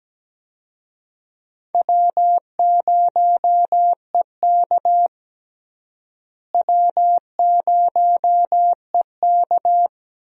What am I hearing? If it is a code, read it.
Morse code, heard as a keyed tone, about 17 wpm, 700 Hz. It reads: W0EK W0EK
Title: USA Call Signs - 4 Characters - Repeated 3x - 17wpm